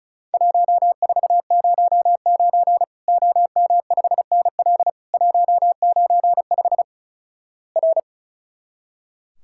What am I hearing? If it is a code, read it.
Morse code, heard as a keyed tone, about 36 wpm, 700 Hz. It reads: 1409 OM5NL 195 R
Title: Worked All Europe DX Contest - QTC Exchange - 35wpm